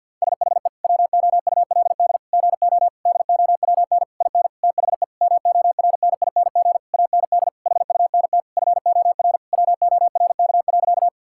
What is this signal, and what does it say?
Morse code, heard as a keyed tone, about 50 wpm, 700 Hz. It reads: SHE WOULD GO DOWN IN THE MORNING AND HUNT FOR WORK.
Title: Sentences from Top 1600 Words 50wpm